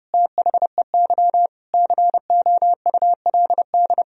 Morse code, heard as a keyed tone, about 30 wpm, 700 Hz. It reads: THEY COULD
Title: Sentences from Top 1000 Words - Review Part 2 - 30wpm